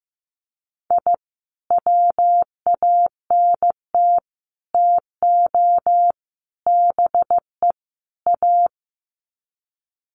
Morse code, heard as a keyed tone, about 15 wpm, 700 Hz. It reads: I WANT TO BE A
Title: Sets of 5 Words Encore 15wpm